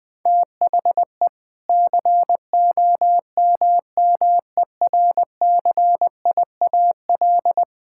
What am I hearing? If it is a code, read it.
Morse code, heard as a keyed tone, about 20 wpm, 700 Hz. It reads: THE COMMERCIAL
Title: Sentences from Top 1000 Words - Review Part 5 - 20wpm